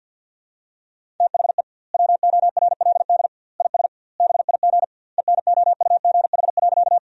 Morse code, heard as a keyed tone, about 50 wpm, 700 Hz. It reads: THE WORLD IS BIG ENOUGH.
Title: 2x Word-Spacing - Sentences from Top 400 Words - 50wpm